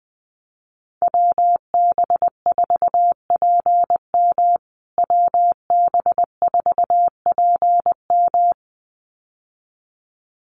Morse code, heard as a keyed tone, about 20 wpm, 700 Hz. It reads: WB4PM WB4PM
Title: USA Call Signs Repeated 2x 20wpm